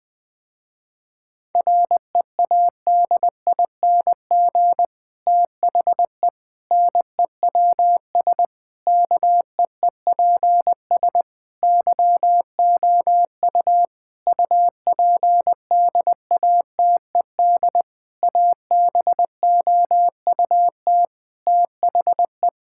Morse code, heard as a keyed tone, about 20 wpm, 700 Hz. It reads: READING THE NEWS KEEPS YOU UPDATED ABOUT THE
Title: Difficult Sentences with Hints - 20wpm